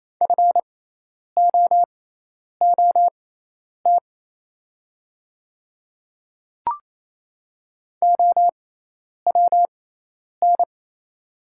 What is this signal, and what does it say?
Morse code, heard as a keyed tone, about 27 wpm, spaced as for 10 wpm, 690 Hz. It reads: FOOT OWN
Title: 6x Character Spacing - Top 500 Words - 28wpm